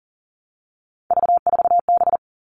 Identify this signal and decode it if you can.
Morse code, heard as a keyed tone, about 40 wpm, 700 Hz. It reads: V4B
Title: All Cut Numbers - 3-Digit 40wpm